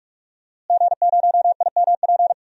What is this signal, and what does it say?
Morse code, heard as a keyed tone, about 44 wpm, 700 Hz. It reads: G0IMP
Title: DXCC-100 Call Signs - 5 Characters - Repeated 2x - 45wpm